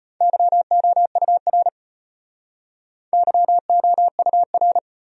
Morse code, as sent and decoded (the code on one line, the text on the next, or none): -.-- --- ..- .-. / -.-- --- ..- .-.
YOUR YOUR